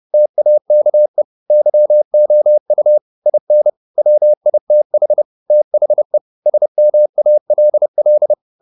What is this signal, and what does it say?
Morse code, heard as a keyed tone, about 30 wpm, 600 Hz. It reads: TAKE YOU IN WITH THE SMALL
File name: the old man and the sea / the old man and the sea 1 30wpm